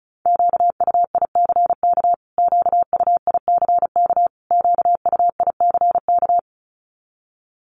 Morse code, heard as a keyed tone, about 34 wpm, 700 Hz. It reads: QUICK QUICK QUICK
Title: Top 500 Words Repeated 35wpm